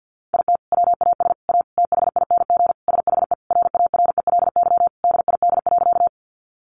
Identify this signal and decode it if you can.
Morse code, heard as a keyed tone, about 50 wpm, 700 Hz. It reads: IT WAS A THING SHE RARELY DID.
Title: Sentences from Top 1400 Words 50wpm